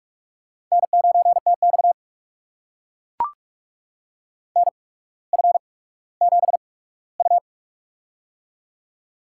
Morse code, heard as a keyed tone, about 44 wpm, spaced as for 19 wpm, 700 Hz. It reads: N9TX NF7U
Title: USA Call Signs - 4 Characters - 8x Character Spacing - 45wpm